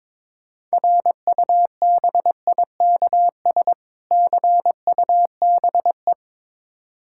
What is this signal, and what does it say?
Morse code, heard as a keyed tone, about 22 wpm, 700 Hz. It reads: RUBIKS CUBE
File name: Difficult Words with Hints - 22wpm